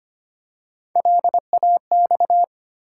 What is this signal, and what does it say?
Morse code, heard as a keyed tone, about 25 wpm, 700 Hz. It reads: LAX
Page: morse-code-ninja-practice-25wpm